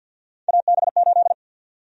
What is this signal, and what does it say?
Morse code, heard as a keyed tone, about 47 wpm, 700 Hz. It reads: AB8
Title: USA Call Sign Prefixes - 50wpm